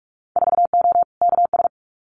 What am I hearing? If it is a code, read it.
Morse code, heard as a keyed tone, about 44 wpm, 690 Hz. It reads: 4O KS
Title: Winter Field Day Exchanges Encore 45wpm